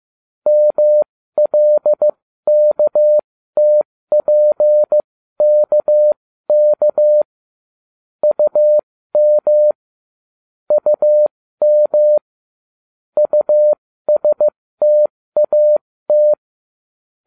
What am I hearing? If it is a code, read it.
Morse code, heard as a keyed tone, about 15 wpm, spaced as for 13 wpm, 600 Hz. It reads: MLKTPKK UM UM USTAT